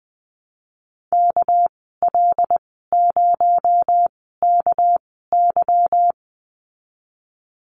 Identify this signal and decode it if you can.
Morse code, heard as a keyed tone, about 20 wpm, spaced as for 15 wpm, 700 Hz. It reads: KL0KY
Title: 238 Lesson 38g - Letter q - Call Signs 20wpm